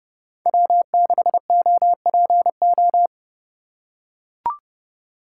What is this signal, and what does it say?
Morse code, heard as a keyed tone, about 30 wpm, 700 Hz. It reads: W6OPO
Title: K1USN SST QSO Practice - 30wpm